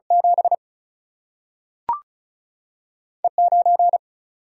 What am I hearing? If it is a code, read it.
Morse code, heard as a keyed tone, about 36 wpm, 700 Hz. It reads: 7 E 9
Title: Single-Letter Single-Number 35wpm